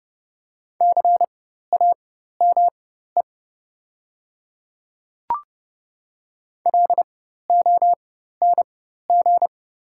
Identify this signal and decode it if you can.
Morse code, heard as a keyed tone, about 30 wpm, spaced as for 14 wpm, 700 Hz. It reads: CAME LONG
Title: 4x Character Spacing - Top 200 Words - 30wpm